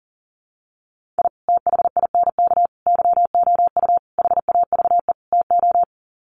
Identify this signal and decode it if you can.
Morse code, heard as a keyed tone, about 40 wpm, 700 Hz. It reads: I THINK YOU HAVE TO
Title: Sets of 5 Words 40wpm